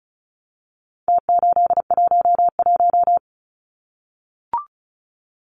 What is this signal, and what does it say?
Morse code, heard as a keyed tone, about 34 wpm, 700 Hz. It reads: T811
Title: Numbers - Contest Style - 4-Digit 35wpm